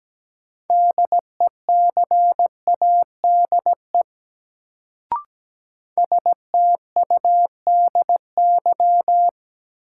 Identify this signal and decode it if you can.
Morse code, heard as a keyed tone, about 17 wpm, 700 Hz. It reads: DECADE STUDY
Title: Sets of 5 Words - New Word Drill - 17wpm